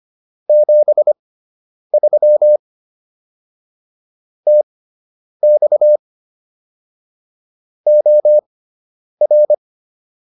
Morse code, heard as a keyed tone, about 25 wpm, spaced as for 9 wpm, 600 Hz. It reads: 73 TX OR